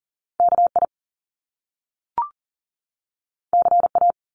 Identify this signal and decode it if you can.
Morse code, heard as a keyed tone, about 40 wpm, 700 Hz. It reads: KI CA
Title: Top Two Letter Combinations in Words 40wpm